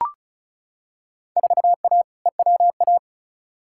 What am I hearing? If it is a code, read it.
Morse code, heard as a keyed tone, about 36 wpm, 700 Hz. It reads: 4A EWA
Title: Field Day Exchanges Encore - 35wpm